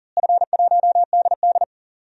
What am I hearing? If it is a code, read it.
Morse code, heard as a keyed tone, about 40 wpm, 700 Hz. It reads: F1DD
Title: DXCC-100 Call Signs - 4 Characters - Repeated 2x - 40wpm